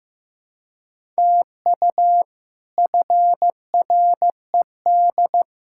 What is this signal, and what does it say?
Morse code, heard as a keyed tone, about 15 wpm, 700 Hz. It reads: TU FRED
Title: K1USN SST QSO Practice - 15wpm